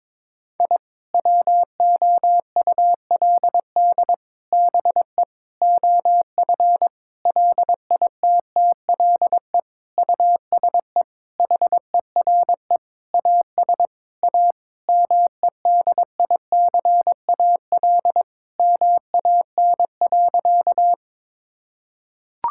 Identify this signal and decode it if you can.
Morse code, heard as a keyed tone, about 22 wpm, 700 Hz. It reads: I WOULD BE OF LITTLE USE HERE AS A MEDICAL MAN.
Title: Sentences from Top 1000 Words - Review Part 5 - 22wpm